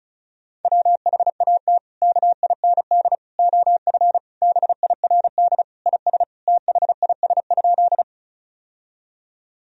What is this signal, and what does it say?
Morse code, heard as a keyed tone, about 36 wpm, 700 Hz. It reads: WHAT KIND OF BIRD IS THIS?
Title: Sentences from Top 1000 Words - Review Part 2 - 35wpm